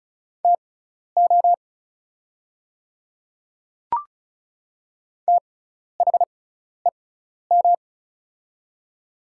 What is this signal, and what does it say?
Morse code, heard as a keyed tone, about 36 wpm, spaced as for 12 wpm, 700 Hz. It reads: TO THEM
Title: 6x Character Spacing - Top 100 Words - 35wpm